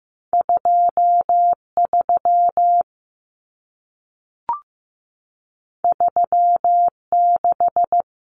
Morse code, heard as a keyed tone, about 15 wpm, 700 Hz. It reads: 2 3 3 6
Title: Numbers - 2-Digit 15wpm